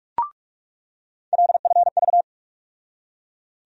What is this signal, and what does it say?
Morse code, heard as a keyed tone, about 42 wpm, 700 Hz. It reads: LFV
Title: USA Call Sign Postfixes - 3 Letters 45wpm